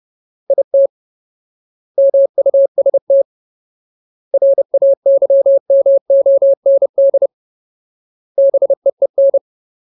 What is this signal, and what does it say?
Morse code, heard as a keyed tone, about 30 wpm, 550 Hz. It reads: IT MUST RAYMOND BEEN
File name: Ultimate Word List / 30WPM Ultimate